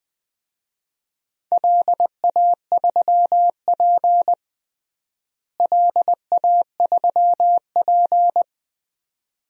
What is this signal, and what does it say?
Morse code, heard as a keyed tone, about 20 wpm, 700 Hz. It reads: LA3P LA3P
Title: DXCC-100 Call Signs - 4 Characters - Repeated 2x - 20wpm